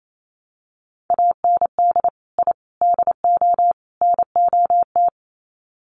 Morse code, heard as a keyed tone, about 28 wpm, 700 Hz. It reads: AND I DO NOT
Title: Sets of 4 Words 28wpm